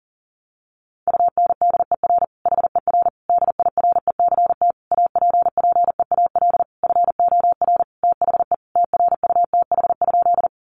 Morse code, heard as a keyed tone, about 40 wpm, 700 Hz. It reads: UNDER HER DIRECT APPEAL FOR THE TRUTH?
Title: Sentences from Top 1300 Words 40wpm